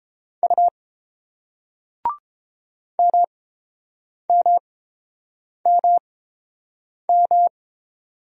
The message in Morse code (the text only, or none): U MMMM